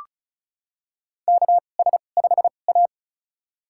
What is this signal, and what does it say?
Morse code, heard as a keyed tone, about 36 wpm, spaced as for 25 wpm, 700 Hz. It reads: KS5A
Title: USA Call Signs - 4 Characters - 2x Character Spacing - 35wpm